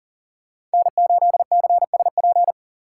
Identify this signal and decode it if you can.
Morse code, heard as a keyed tone, about 40 wpm, 700 Hz. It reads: N8CSP
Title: USA Call Signs - 5 Characters - 6x Character Spacing - 40wpm